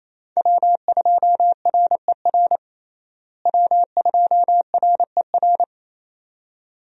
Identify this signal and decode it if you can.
Morse code, heard as a keyed tone, about 27 wpm, 700 Hz. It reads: W2RER W2RER
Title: USA Call Signs - 5 Characters - Repeated 2x - 28wpm